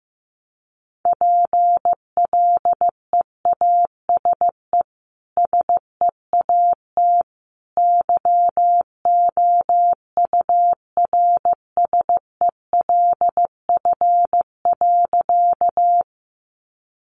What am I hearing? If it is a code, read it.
Morse code, heard as a keyed tone, about 15 wpm, 700 Hz. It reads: PLEASE SEAT YOURSELF.
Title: Sentences from Top 1500 Words 15wpm